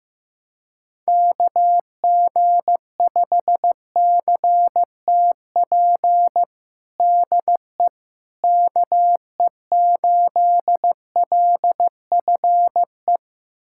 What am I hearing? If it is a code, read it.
Morse code, heard as a keyed tone, about 15 wpm, 700 Hz. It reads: KG5CTP DE KE8LFE
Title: USA Callsign DE Callsign - 6 Characters - 15wpm